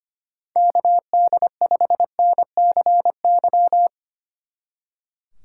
Morse code, heard as a keyed tone, about 30 wpm, 700 Hz. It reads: KD5NCY
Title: Speed Racing - USA Call Signs 25wpm